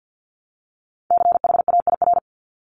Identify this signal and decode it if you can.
Morse code, heard as a keyed tone, about 50 wpm, 700 Hz. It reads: CHAIR
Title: Multi-letter Phonemes Repeated 3x 50wpm